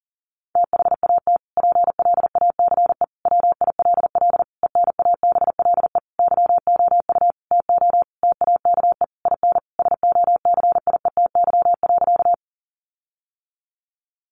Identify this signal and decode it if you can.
Morse code, heard as a keyed tone, about 40 wpm, 700 Hz. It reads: THAT PLACE WILL ENABLE YOU TO TAKE IN SOCIETY.
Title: Sentences from Top 1500 Words 40wpm